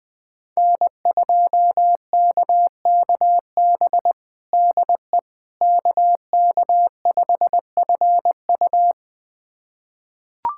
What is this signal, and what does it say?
Morse code, heard as a keyed tone, about 21 wpm, 700 Hz. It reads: N2KKB DE KK5FU
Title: USA Callsign DE Callsign 20wpm